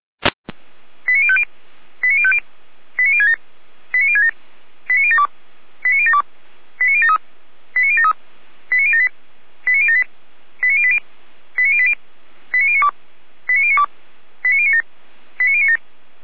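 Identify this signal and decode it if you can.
Five-tone selective call (ZVEI), heard as quick runs of five tones, tones of about 70 ms. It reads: 89050 89050 89067 89067 89072 89072 89073 89073 89078 89078 89080 89080 89092 89092 89097 89097
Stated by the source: ZVEI five-tone selective call received on an SDR.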